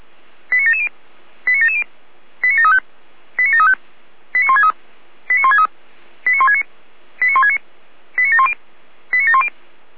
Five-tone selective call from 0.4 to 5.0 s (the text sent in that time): 89770 89770 89735 89735 89163